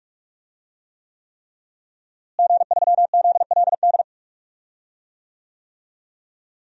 Morse code, heard as a keyed tone, about 42 wpm, 700 Hz. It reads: G3ZLD
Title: DXCC-100 Call Signs - 5 Characters - Repeated 3x - 45wpm